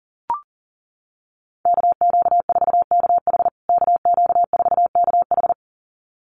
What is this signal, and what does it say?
Morse code, heard as a keyed tone, about 40 wpm, 700 Hz. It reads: KQ4KH KQ4KH
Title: USA Call Signs Repeated 2x 40wpm